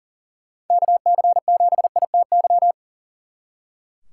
Morse code, heard as a keyed tone, about 49 wpm, 700 Hz. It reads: KC7ITY
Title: Speed Racing - USA 2x3 Call Signs 40wpm